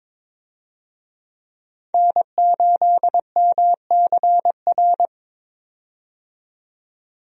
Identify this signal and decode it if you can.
Morse code, heard as a keyed tone, about 22 wpm, 700 Hz. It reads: N8MCR
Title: USA Call Signs - 5 Characters - 22wpm